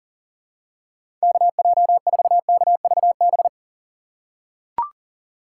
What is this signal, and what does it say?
Morse code, heard as a keyed tone, about 37 wpm, 700 Hz. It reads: KJ4KVB
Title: USA Call Signs - 6 Characters - 6x Character Spacing - 40wpm